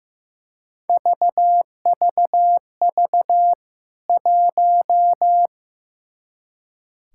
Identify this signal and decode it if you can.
Morse code, heard as a keyed tone, about 15 wpm, 700 Hz. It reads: VVV 1
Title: K1USN SST QSO Practice 15wpm